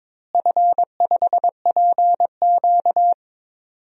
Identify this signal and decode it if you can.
Morse code, heard as a keyed tone, about 22 wpm, 700 Hz. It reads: F5PQ
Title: DXCC-100 Call Signs - 4 Characters - Repeated 3x - 22wpm